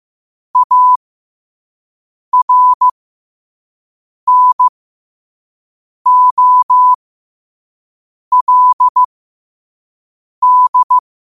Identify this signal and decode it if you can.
Morse code, heard as a keyed tone, about 15 wpm, spaced as for 5 wpm, 1000 Hz. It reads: ARNOLD
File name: 5wpm